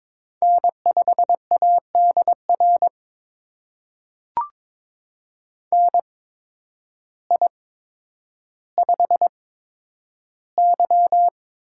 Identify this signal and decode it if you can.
Morse code, heard as a keyed tone, about 22 wpm, spaced as for 10 wpm, 700 Hz. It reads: N5ADR NI5Y
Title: USA Call Signs - 8x Character Spacing - 22wpm